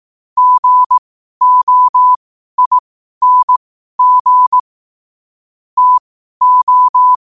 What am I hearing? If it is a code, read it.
Morse code, heard as a keyed tone, about 18 wpm, spaced as for 12 wpm, 1000 Hz. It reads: GOING TO
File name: qotdfiles / 12wpm